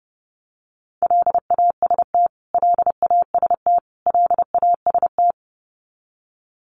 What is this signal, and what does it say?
Morse code, heard as a keyed tone, about 30 wpm, 700 Hz. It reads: LAST LAST LAST
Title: Top 500 Words Repeated 30wpm